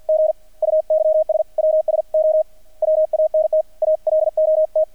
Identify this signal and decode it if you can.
Morse code, heard as a keyed tone, about 50 wpm, spaced as for 42 wpm, 630 Hz. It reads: GUYS WS O WANT ALOT
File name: archived / CRSnet